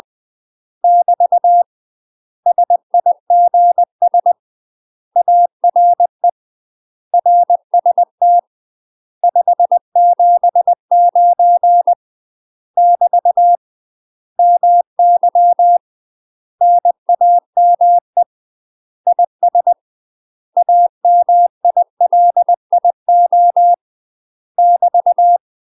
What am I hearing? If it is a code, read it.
Morse code, heard as a keyed tone, about 20 wpm, 700 Hz. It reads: <BT> SIGS ARE RST 579 <BT> MY NAME IS AMILIO <BT>
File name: UK MASTER QSOs TRAINING / 20WPM / UK MASTER QSOs 0057